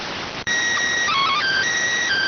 SSTV (slow-scan television), PD 290.